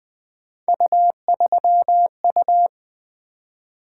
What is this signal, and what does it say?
Morse code, heard as a keyed tone, about 20 wpm, 700 Hz. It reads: U3U
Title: Three Letter-Number Combinations - 20wpm